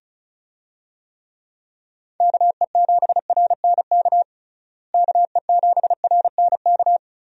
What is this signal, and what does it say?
Morse code, heard as a keyed tone, about 34 wpm, 700 Hz. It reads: KE7RNK KE7RNK
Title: USA Call Signs - 6 Characters - Repeated 2x - 35wpm